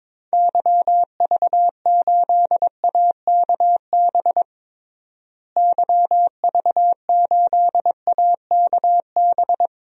Morse code, heard as a keyed tone, about 22 wpm, 700 Hz. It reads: YV8AKB YV8AKB
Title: DXCC-100 Call Signs - 6 Characters - Repeated 3x - 22wpm